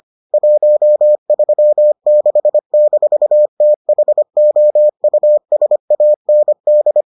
Morse code, heard as a keyed tone, about 25 wpm, 600 Hz. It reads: 136-THOUSAND